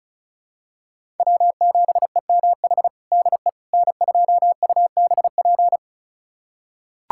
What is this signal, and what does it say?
Morse code, heard as a keyed tone, about 36 wpm, 700 Hz. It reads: W7EMH DE N2UBP
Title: USA Callsign DE Callsign - 5 Characters - 35wpm